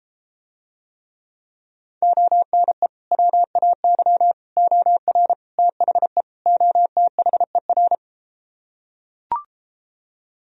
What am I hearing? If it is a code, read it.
Morse code, heard as a keyed tone, about 34 wpm, 700 Hz. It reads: ONE WAY OR THE OTHER